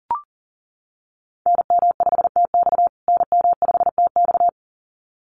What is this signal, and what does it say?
Morse code, heard as a keyed tone, about 40 wpm, 700 Hz. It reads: NM5TX NM5TX
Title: USA Call Signs Repeated 2x 40wpm